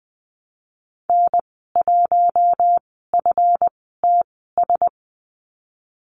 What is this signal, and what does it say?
Morse code, heard as a keyed tone, about 20 wpm, spaced as for 15 wpm, 700 Hz. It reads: N1FTS
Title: 179 Lesson 29g - Slash - Call Signs 20wpm